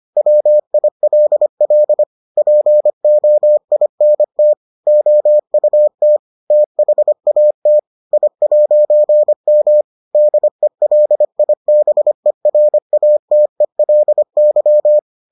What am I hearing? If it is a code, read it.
Morse code, heard as a keyed tone, about 25 wpm, 600 Hz. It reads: WILL POINT OUT THAT I'M DELIBERATELY